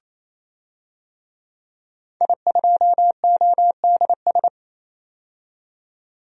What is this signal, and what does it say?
Morse code, heard as a keyed tone, about 27 wpm, 700 Hz. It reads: I2ODS
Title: DXCC-100 Call Signs - 5 Characters - 28wpm